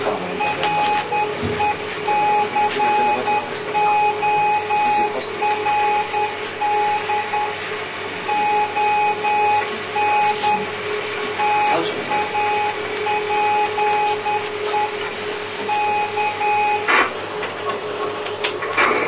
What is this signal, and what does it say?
Morse code, heard as a keyed tone, about 10 wpm, 840 Hz. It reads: RECORD ON TAPE K